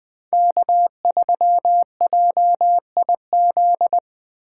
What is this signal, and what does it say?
Morse code, heard as a keyed tone, about 21 wpm, 700 Hz. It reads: K3JIZ